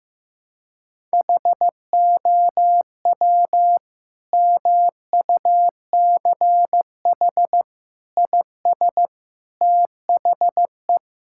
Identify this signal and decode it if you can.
Morse code, heard as a keyed tone, about 15 wpm, 700 Hz. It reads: HOW MUCH IS THE
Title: Sentences from Top 1000 Words - Review Part 5 - 15wpm